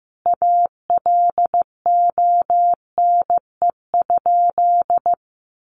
Morse code, heard as a keyed tone, about 15 wpm, 700 Hz. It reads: ALONE?
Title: Sentences from Top 1000 Words 15wpm